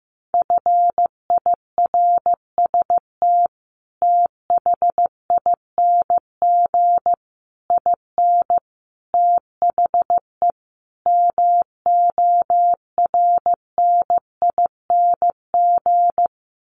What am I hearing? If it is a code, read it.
Morse code, heard as a keyed tone, about 15 wpm, 700 Hz. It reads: FIRST THING IN THE MORNING
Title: Sets of 5 Words Encore2 15wpm